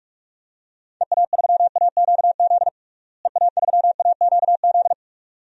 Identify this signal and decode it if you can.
Morse code, heard as a keyed tone, about 41 wpm, 700 Hz. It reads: EA3AQZ EA3AQZ
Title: DXCC-100 Call Signs - 6 Characters - Repeated 2x - 45wpm